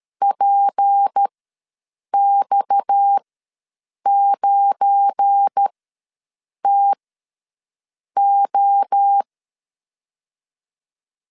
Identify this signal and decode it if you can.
Morse code, heard as a keyed tone, about 13 wpm, spaced as for 6 wpm, 800 Hz. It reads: PX9TO
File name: LEZIONI / 7 GF9Z / 7 GF9Z ws 12wpm 3321